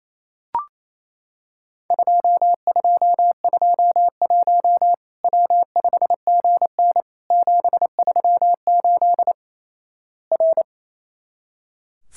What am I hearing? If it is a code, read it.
Morse code, heard as a keyed tone, about 28 wpm, 700 Hz. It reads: 2221 W5GN 738 R
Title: Worked All Europe DX Contest - QTC Exchange - 28wpm